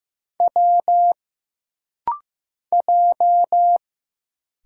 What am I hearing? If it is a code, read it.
Morse code, heard as a keyed tone, about 15 wpm, 700 Hz. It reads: W J
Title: Single Letters - Rapid-Fire 15wpm